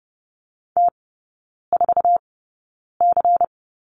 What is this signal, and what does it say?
Morse code, heard as a keyed tone, about 30 wpm, spaced as for 9 wpm, 690 Hz. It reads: T4C
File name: Lesson 16e - Letter c - Three Characters 30wpm